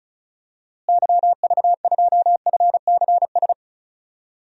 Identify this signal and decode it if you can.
Morse code, heard as a keyed tone, about 36 wpm, 700 Hz. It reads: YV2FCS
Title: DXCC-100 Call Signs - 6 Characters - Repeated 2x - 35wpm